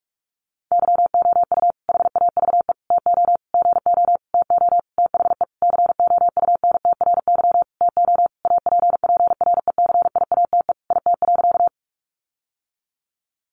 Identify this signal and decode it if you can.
Morse code, heard as a keyed tone, about 44 wpm, 700 Hz. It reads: YOU HAVE TO GO TO THE COUNTRY TO APPRECIATE IT.
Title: Sentences from Top 1300 Words 45wpm